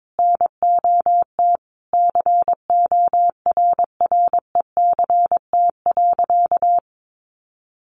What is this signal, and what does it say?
Morse code, heard as a keyed tone, about 22 wpm, 700 Hz. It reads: NOT CORRECT.
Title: Sentences from Top 1200 Words 22wpm